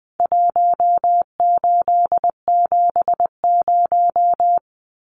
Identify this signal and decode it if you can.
Morse code, heard as a keyed tone, about 20 wpm, 700 Hz. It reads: 1870
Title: Numbers - Contest Style - 4-Digit 20wpm